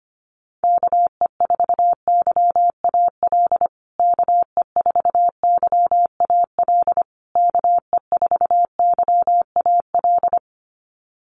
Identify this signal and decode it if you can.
Morse code, heard as a keyed tone, about 25 wpm, 700 Hz. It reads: KE4YAL KE4YAL KE4YAL
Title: USA Call Signs Repeated 3x 25wpm